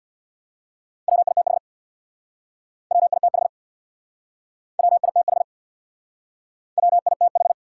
Morse code, heard as a keyed tone, about 64 wpm, 700 Hz. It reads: WITH WITH WITH WITH